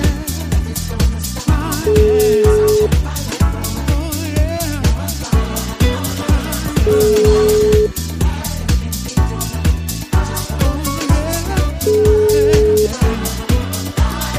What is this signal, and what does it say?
Ringback tone (Europe, 425 Hz), with noise.